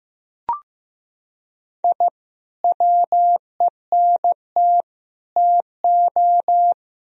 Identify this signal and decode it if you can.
Morse code, heard as a keyed tone, about 15 wpm, 700 Hz. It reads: I WENT TO
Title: Sentences from Top 1000 Words - Review Part 1 - 15wpm